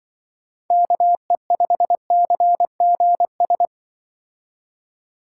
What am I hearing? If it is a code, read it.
Morse code, heard as a keyed tone, about 24 wpm, 700 Hz. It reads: KE5CGS
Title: Speed Racing - USA 2x3 Call Signs 20wpm